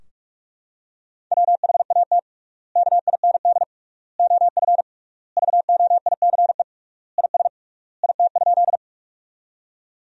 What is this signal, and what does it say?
Morse code, heard as a keyed tone, about 46 wpm, 700 Hz. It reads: WHAT KIND OF VOICE IS IT?
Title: 3x Word-Spacing - Sentences from Top 500 Words - 45wpm